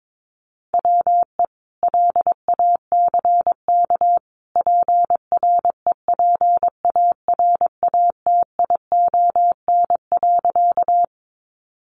Morse code, heard as a keyed tone, about 22 wpm, 700 Hz. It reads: WE LACK PREPARATION.
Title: Sentences from Top 1400 Words 22wpm